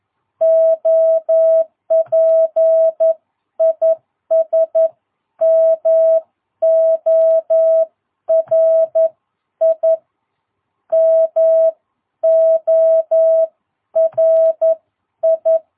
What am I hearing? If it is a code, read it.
Morse code, heard as a keyed tone, about 11 wpm, spaced as for 9 wpm, 660 Hz. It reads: OP IS MORI MORI
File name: jh8cbh / 2022cw